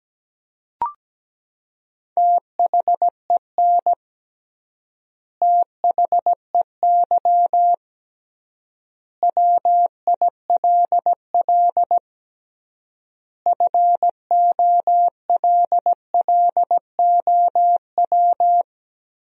Morse code, heard as a keyed tone, about 17 wpm, 700 Hz. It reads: THEN THEY WILL FOLLOW
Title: 3x Word-Spacing - Sentences from Top 200 Words - 17wpm